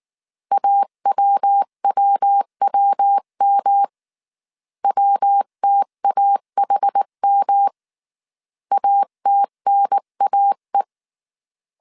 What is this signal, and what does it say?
Morse code, heard as a keyed tone, about 19 wpm, 800 Hz. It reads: AWWWM WTAHM ATNAE